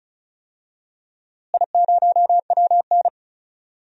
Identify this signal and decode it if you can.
Morse code, heard as a keyed tone, about 36 wpm, 700 Hz. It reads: I0WN
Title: DXCC-100 Call Signs - 4 Characters - Repeated 3x - 35wpm